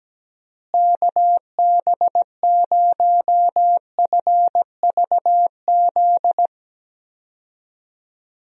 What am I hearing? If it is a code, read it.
Morse code, heard as a keyed tone, about 17 wpm, 700 Hz. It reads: KB0FVZ
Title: USA Call Signs - 6 Characters - Repeated 3x - 17wpm